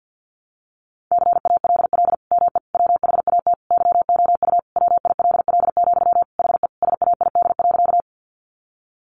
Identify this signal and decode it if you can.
Morse code, heard as a keyed tone, about 50 wpm, 700 Hz. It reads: CALL ME WHAT YOU WILL, HE SAID.
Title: Sentences from Top 200 Words 50wpm